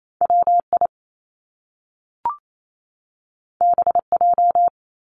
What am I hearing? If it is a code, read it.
Morse code, heard as a keyed tone, about 28 wpm, 700 Hz. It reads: WI BJ
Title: Top Two Letter Combinations in Words 28wpm